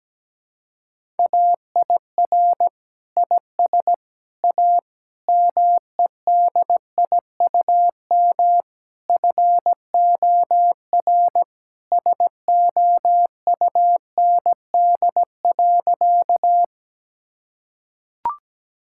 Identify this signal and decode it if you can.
Morse code, heard as a keyed tone, about 17 wpm, 700 Hz. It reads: AIR IS A MEDIUM FOR SOUND.
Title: Sentences from Top 1000 Words - Review Part 6 - 17wpm